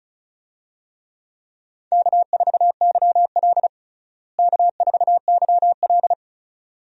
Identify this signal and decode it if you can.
Morse code, heard as a keyed tone, about 34 wpm, 700 Hz. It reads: K4YL K4YL
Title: USA Call Signs - 4 Characters - Repeated 2x - 35wpm